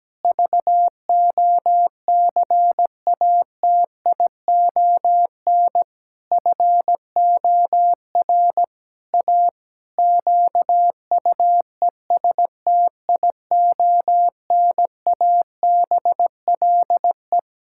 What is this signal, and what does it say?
Morse code, heard as a keyed tone, about 17 wpm, 700 Hz. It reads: VOCATION FOR A QUESTIONABLE
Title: Pangram Challenge v2 - 17wpm